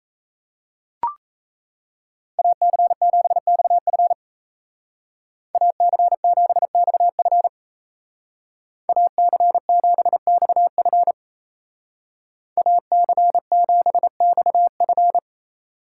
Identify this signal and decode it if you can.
Morse code, heard as a keyed tone, about 37 wpm, 700 Hz. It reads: AC7XF AC7XF AC7XF AC7XF